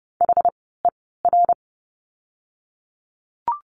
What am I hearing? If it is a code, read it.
Morse code, heard as a keyed tone, about 30 wpm, spaced as for 17 wpm, 700 Hz. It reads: HER
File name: Lesson 17f - Letter u - Words 30wpm